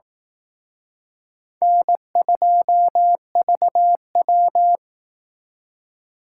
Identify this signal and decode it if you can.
Morse code, heard as a keyed tone, about 18 wpm, 700 Hz. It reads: N2VW
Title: Speed Racing - USA 1x2 Call Signs 15wpm